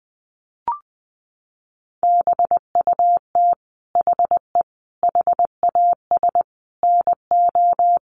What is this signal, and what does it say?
Morse code, heard as a keyed tone, about 20 wpm, 700 Hz. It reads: BUT HE HAS NO
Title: Sentences from Top 1300 Words 20wpm